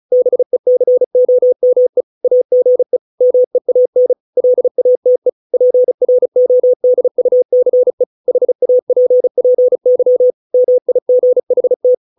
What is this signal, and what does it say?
Morse code, heard as a keyed tone, about 36 wpm, 500 Hz. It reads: BECOME AGE MEAN LATE PRODUCE HAPPY MIGHT